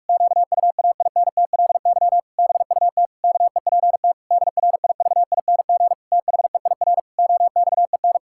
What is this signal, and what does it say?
Morse code, heard as a keyed tone, about 46 wpm, 700 Hz. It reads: QUAINTLY BUT KEPT DRIVING THEIR OXEN
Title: Pangram Challenge v2 - 45wpm